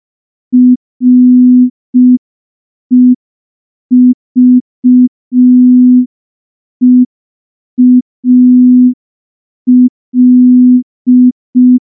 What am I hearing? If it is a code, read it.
Morse code, heard as a keyed tone, about 5 wpm, 250 Hz. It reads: REVEAL
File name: morse / morse-training-05WPM